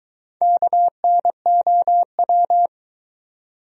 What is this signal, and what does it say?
Morse code, heard as a keyed tone, about 23 wpm, 700 Hz. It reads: KNOW